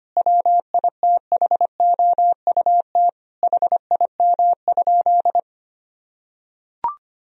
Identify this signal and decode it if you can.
Morse code, heard as a keyed tone, about 25 wpm, 700 Hz. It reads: WITHOUT HIM?
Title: Sentences from Top 1000 Words - Review Part 3 - 25wpm